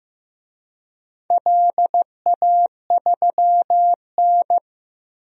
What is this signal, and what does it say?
Morse code, heard as a keyed tone, about 15 wpm, 700 Hz. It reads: LA3N
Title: DXCC-100 Call Signs - 4 Characters - Repeated 2x - 15wpm